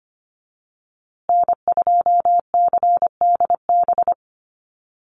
Morse code, heard as a keyed tone, about 25 wpm, 700 Hz. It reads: N2CDB
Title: USA Call Signs Repeated 2x 25wpm